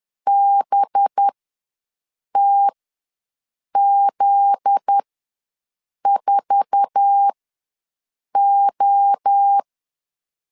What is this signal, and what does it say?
Morse code, heard as a keyed tone, about 10 wpm, spaced as for 6 wpm, 800 Hz. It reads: BTZ4O
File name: LEZIONI / 8 YJ45 / 8 YJ45 ws 10wpm 3843